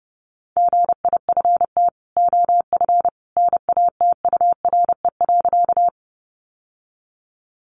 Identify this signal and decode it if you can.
Morse code, heard as a keyed tone, about 30 wpm, 700 Hz. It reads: GIFT OF NATURE.
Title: Sentences from Top 900 Words 30wpm